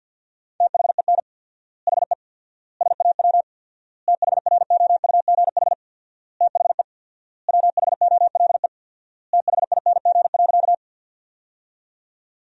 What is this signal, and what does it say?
Morse code, heard as a keyed tone, about 50 wpm, 700 Hz. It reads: THEN HE SAW THROUGH THE WHOLE THING.
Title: 4x Word-Spacing - Sentences from Top 400 Words - 50wpm